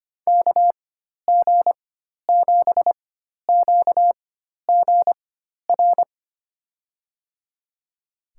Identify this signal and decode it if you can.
Morse code, heard as a keyed tone, about 25 wpm, spaced as for 12 wpm, 700 Hz. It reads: KG7QGR
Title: USA Call Signs - 6 Characters - 4x Character Spacing - 25wpm